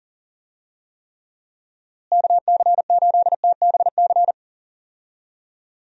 Morse code, heard as a keyed tone, about 37 wpm, 700 Hz. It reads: KC8TBC
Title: USA Call Signs - 6 Characters - Repeated 2x - 40wpm